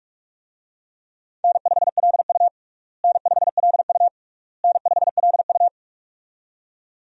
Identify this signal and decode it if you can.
Morse code, heard as a keyed tone, about 46 wpm, 700 Hz. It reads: N5LU N5LU N5LU
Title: USA Call Signs - 4 Characters - Repeated 3x - 45wpm